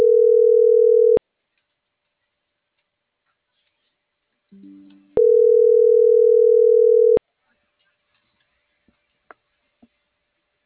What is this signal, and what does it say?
Ringback tone (North America), 440 + 480 Hz.